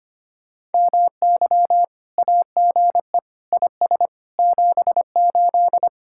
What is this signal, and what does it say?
Morse code, heard as a keyed tone, about 33 wpm, 700 Hz. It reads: MY AGE IS 78